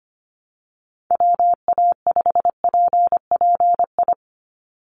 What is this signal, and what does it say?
Morse code, heard as a keyed tone, about 25 wpm, 700 Hz. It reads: WA5PPI
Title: USA Call Signs 25wpm